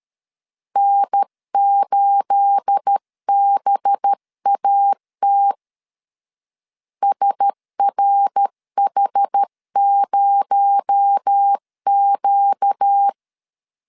Morse code, heard as a keyed tone, about 13 wpm, 800 Hz. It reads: N8BAT SRH0Q
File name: LEZIONI / 9 286 12wpm 2305